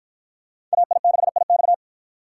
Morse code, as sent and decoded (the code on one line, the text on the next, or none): .- .. -.... .. -..-
AI6IX